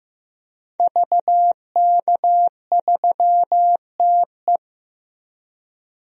Morse code, heard as a keyed tone, about 15 wpm, 700 Hz. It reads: VK3TE
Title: DXCC-100 Call Signs - 5 Characters - Repeated 3x - 15wpm